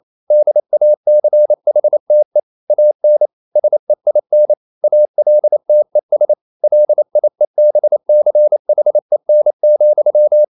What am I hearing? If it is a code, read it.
Morse code, heard as a keyed tone, about 28 wpm, 600 Hz. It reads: DACHTE AN SEIN ALTES LIEBCHEN,
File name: Liebespaar4 28wpm